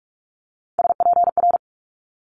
Morse code, heard as a keyed tone, about 46 wpm, 700 Hz. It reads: SPR
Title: Multi-letter Phonemes Quick Fire 45wpm